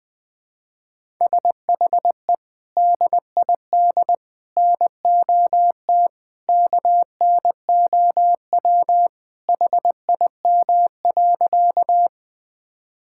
Morse code, heard as a keyed tone, about 19 wpm, 700 Hz. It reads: SHE DID NOT KNOW HIM.
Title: Sentences with 3-Word Phrases - Spoken Beginning - 20wpm